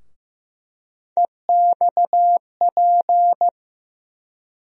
Morse code, heard as a keyed tone, about 15 wpm, 700 Hz. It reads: EXP
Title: USA Call Sign Postfixes - 3 Letters 15wpm